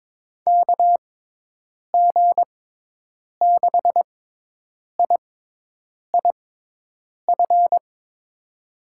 Morse code, heard as a keyed tone, about 22 wpm, spaced as for 8 wpm, 700 Hz. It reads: KG6IIF
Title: USA Call Signs - 6 Characters - 6x Character Spacing - 22wpm